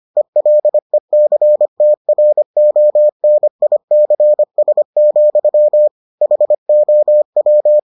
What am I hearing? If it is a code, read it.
Morse code, heard as a keyed tone, about 25 wpm, 600 Hz. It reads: ELECTRONICS, HOW